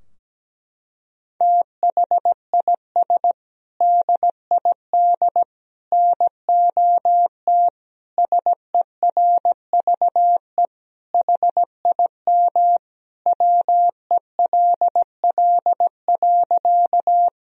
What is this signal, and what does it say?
Morse code, heard as a keyed tone, about 17 wpm, 700 Hz. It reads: THIS DID NOT SERVE HIM WELL.
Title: Sentences from Top 1000 Words - Review Part 2 - 17wpm